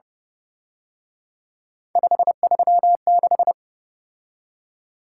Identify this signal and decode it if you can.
Morse code, heard as a keyed tone, about 29 wpm, 700 Hz. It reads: 536